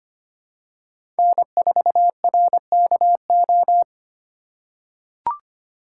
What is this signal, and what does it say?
Morse code, heard as a keyed tone, about 25 wpm, 700 Hz. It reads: N4RKO
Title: USA Call Signs - 5 Characters - 3x Character Spacing - 25wpm